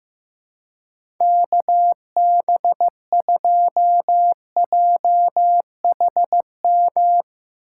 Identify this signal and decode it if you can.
Morse code, heard as a keyed tone, about 15 wpm, 700 Hz. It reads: KB2JHM
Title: USA Call Signs - 6 Characters - 8x Character Spacing - 15wpm